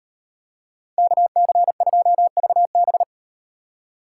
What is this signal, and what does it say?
Morse code, heard as a keyed tone, about 38 wpm, 700 Hz. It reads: KC2VB